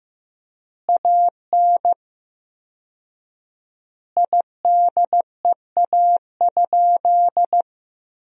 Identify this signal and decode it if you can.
Morse code, heard as a keyed tone, about 16 wpm, 700 Hz. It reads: AN IDEA?
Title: 4x Word-Spacing - Sentences from Top 400 Words - 15wpm